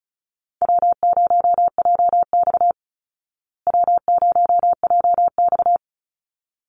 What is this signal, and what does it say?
Morse code, heard as a keyed tone, about 34 wpm, 700 Hz. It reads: W0JX W0JX
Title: Easy Call Signs Repeated 2x - 35wpm